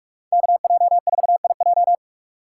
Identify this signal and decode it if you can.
Morse code, heard as a keyed tone, about 42 wpm, 700 Hz. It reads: KJ4IJ
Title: USA Call Signs - 5 Characters - 4x Character Spacing - 45wpm